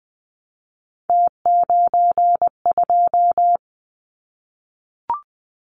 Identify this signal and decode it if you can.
Morse code, heard as a keyed tone, about 20 wpm, 700 Hz. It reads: T92
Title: Callsign Prefixes for Top 50 DXCC - 20wpm